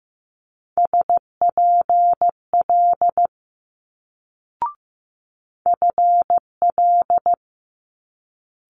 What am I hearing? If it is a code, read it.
Morse code, heard as a keyed tone, about 15 wpm, 700 Hz. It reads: SPL FL
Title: Multi-letter Phonemes Quick Fire 15wpm